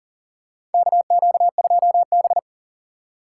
Morse code, heard as a keyed tone, about 40 wpm, 700 Hz. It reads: KQ2B
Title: USA Call Signs - 4 Characters - 6x Character Spacing - 40wpm